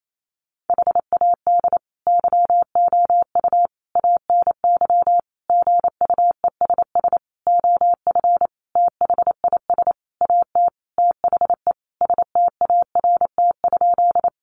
Morse code, heard as a keyed tone, about 28 wpm, 700 Hz. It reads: HAD YOU ANY GUESS OF THIS AT THE START?
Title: Sentences from Top 1500 Words 28wpm